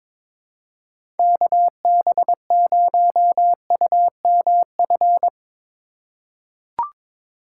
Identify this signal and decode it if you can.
Morse code, heard as a keyed tone, about 21 wpm, 700 Hz. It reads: KB0UMF
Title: USA Call Signs - 6 Characters - 3x Character Spacing - 22wpm